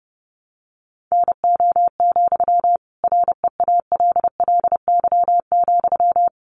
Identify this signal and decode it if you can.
Morse code, heard as a keyed tone, about 30 wpm, 700 Hz. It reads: NO, REALLY,
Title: Sentences from Top 1000 Words 30wpm